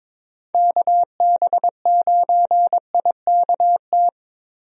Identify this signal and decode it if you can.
Morse code, heard as a keyed tone, about 23 wpm, 700 Hz. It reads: KB9IKT
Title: USA Call Signs - 6 Characters - Repeated 3x - 22wpm